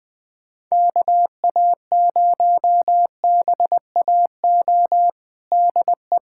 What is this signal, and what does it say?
Morse code, heard as a keyed tone, about 20 wpm, 700 Hz. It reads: KA0BAO DE
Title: USA Callsign DE Callsign 20wpm